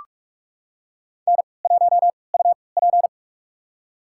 Morse code, heard as a keyed tone, about 44 wpm, spaced as for 25 wpm, 690 Hz. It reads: N1UP
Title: USA Call Signs - 4 Characters - 3x Character Spacing - 45wpm